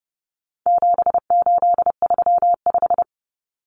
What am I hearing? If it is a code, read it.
Morse code, heard as a keyed tone, about 30 wpm, 700 Hz. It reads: 7835
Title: Numbers - 4-Digit 30wpm